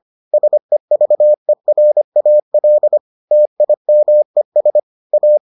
Morse code, heard as a keyed tone, about 25 wpm, 600 Hz. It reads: SEVERAL TIMES A